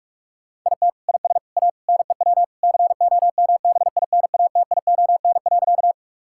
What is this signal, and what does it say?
Morse code, heard as a keyed tone, about 46 wpm, 700 Hz. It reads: IT IS A NEW COMBINATION.
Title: Sentences from Top 1000 Words - Review Part 6 - 45wpm